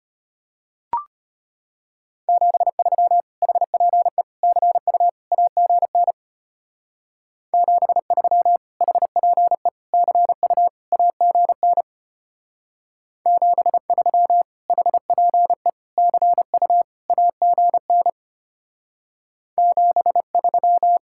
Morse code, heard as a keyed tone, about 30 wpm, 700 Hz. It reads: 73 HPE CU AGN 73 HPE CU AGN 73 HPE CU AGN 73
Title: Speed-Racing Common QSO Phrases - 25wpm